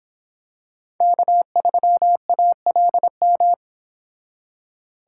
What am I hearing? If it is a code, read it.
Morse code, heard as a keyed tone, about 25 wpm, 700 Hz. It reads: K3ALM